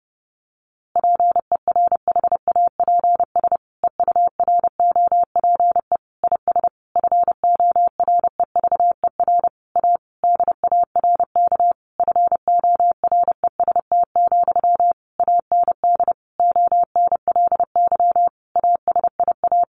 Morse code, heard as a keyed tone, about 30 wpm, 700 Hz. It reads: PERHAPS EUROPE IS FOREVER A DARK FOREST, AND ONLY ASIA
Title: A Sound Of Thunder 30wpm